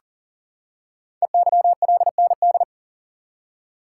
Morse code, heard as a keyed tone, about 41 wpm, 700 Hz. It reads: EYLND
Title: Code Groups - 5 Letters - 40wpm